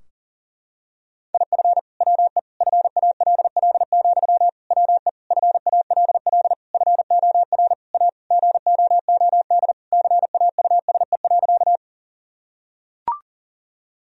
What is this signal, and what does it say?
Morse code, heard as a keyed tone, about 40 wpm, 700 Hz. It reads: IF WE FALL, WE FALL FOR A GOOD CAUSE.
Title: Sentences from Top 1000 Words - Review Part 2 - 40wpm